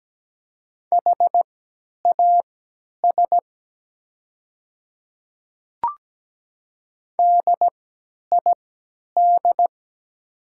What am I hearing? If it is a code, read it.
Morse code, heard as a keyed tone, about 17 wpm, spaced as for 10 wpm, 700 Hz. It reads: HAS DID
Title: 3x Character Spacing - Top 100 Words - 17wpm